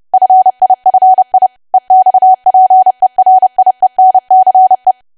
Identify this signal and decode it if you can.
Morse code, heard as a keyed tone, about 30 wpm, 750 Hz. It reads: FIFI EXPERIENCE
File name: cwa / W1AW-Level3 / 30WPM